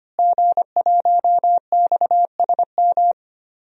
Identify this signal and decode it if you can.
Morse code, heard as a keyed tone, about 25 wpm, 700 Hz. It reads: G1XSM
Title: DXCC-100 Call Signs - 5 Characters - Repeated 2x - 25wpm